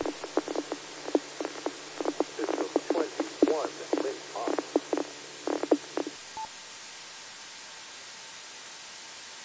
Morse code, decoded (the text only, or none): none